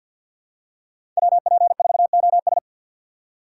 Morse code, heard as a keyed tone, about 50 wpm, 700 Hz. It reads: WP4OS
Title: USA Call Signs - 5 Characters - 6x Character Spacing - 50wpm